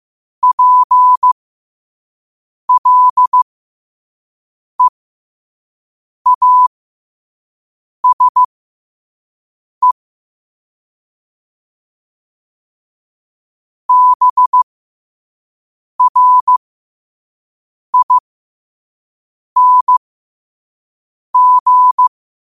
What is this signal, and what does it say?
Morse code, heard as a keyed tone, about 15 wpm, spaced as for 5 wpm, 1000 Hz. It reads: PLEASE BRING
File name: qotdfiles / 5wpm